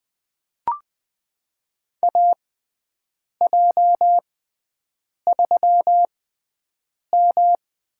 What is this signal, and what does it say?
Morse code, heard as a keyed tone, about 21 wpm, spaced as for 7 wpm, 700 Hz. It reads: AJ3M
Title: USA Call Signs - 6x Character Spacing - 20wpm